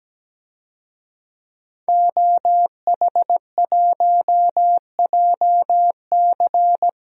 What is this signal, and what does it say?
Morse code, heard as a keyed tone, about 17 wpm, 700 Hz. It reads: OH1JC
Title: DXCC-100 Call Signs - 5 Characters - 17wpm